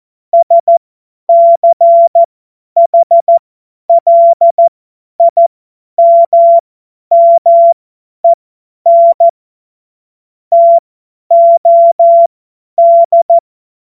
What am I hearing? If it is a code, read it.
Morse code, heard as a keyed tone, about 14 wpm, spaced as for 10 wpm, 680 Hz. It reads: SCHLIMMEN TOD
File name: Morsekurs / Witze / Witz 26. Oktober 2021, 50 F70